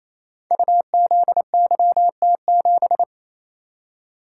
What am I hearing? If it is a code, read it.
Morse code, heard as a keyed tone, about 28 wpm, 690 Hz. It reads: UZYT7
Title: Code Groups - 5 Letter:Numbers - 28wpm